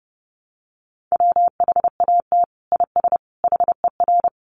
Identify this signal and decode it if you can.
Morse code, heard as a keyed tone, about 30 wpm, 700 Hz. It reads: WHAT IS HER
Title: Sentences from Top 1100 Words 30wpm